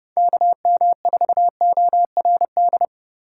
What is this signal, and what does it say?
Morse code, heard as a keyed tone, about 30 wpm, 700 Hz. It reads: KM4ORD
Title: USA Call Signs - 3x Character Spacing - 30wpm